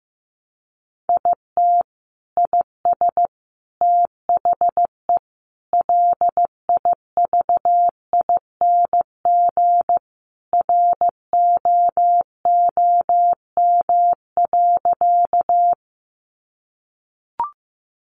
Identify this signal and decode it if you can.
Morse code, heard as a keyed tone, about 15 wpm, 700 Hz. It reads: IT IS THE LIVING ROOM.
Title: Sentences from Top 1100 Words 15wpm